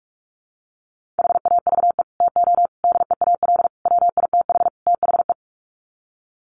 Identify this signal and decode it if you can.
Morse code, heard as a keyed tone, about 44 wpm, 700 Hz. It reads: HAVE TO DEAL WITH THE
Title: Sets of 5 Words Encore2 45wpm